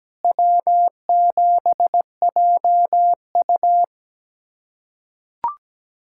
Morse code, heard as a keyed tone, about 17 wpm, 700 Hz. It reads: W7JU
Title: USA Callsign DE Callsign 17wpm